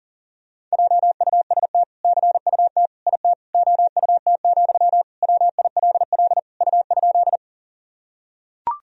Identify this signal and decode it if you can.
Morse code, heard as a keyed tone, about 40 wpm, 700 Hz. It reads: JUST CUT IT OUT, WILL U?
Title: Sentences from Top 300 Words and Abbreviations 40wpm